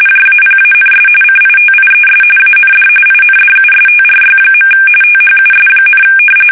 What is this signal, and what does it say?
RTTY, about 45.45 baud. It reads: JAPANS CITIES AND INDUSTRY. THUS,
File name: fsk850 / battleshipnj